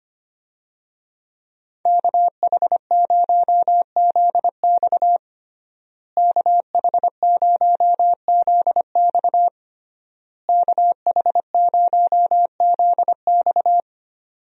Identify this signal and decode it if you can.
Morse code, heard as a keyed tone, about 25 wpm, 700 Hz. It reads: KH0ZX KH0ZX KH0ZX
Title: USA Call Signs - 5 Characters - Repeated 3x - 25wpm